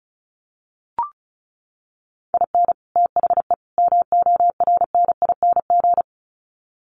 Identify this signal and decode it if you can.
Morse code, heard as a keyed tone, about 34 wpm, 700 Hz. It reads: IN THE MORNING
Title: Sets of 3 Words 35wpm